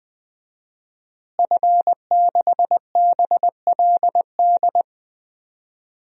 Morse code, heard as a keyed tone, about 20 wpm, 700 Hz. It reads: F6BLD
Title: DXCC-100 Call Signs - 5 Characters - Repeated 2x - 20wpm